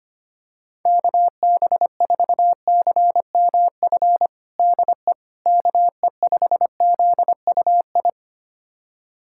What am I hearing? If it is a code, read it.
Morse code, heard as a keyed tone, about 25 wpm, 700 Hz. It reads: KB4CMF DE KE5ZUI
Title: USA Callsign DE Callsign 25wpm